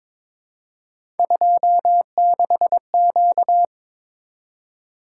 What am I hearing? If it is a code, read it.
Morse code, heard as a keyed tone, about 22 wpm, 700 Hz. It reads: 26Q
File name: practice / 22wpm / Three Letter-Number Combinations - 22wpm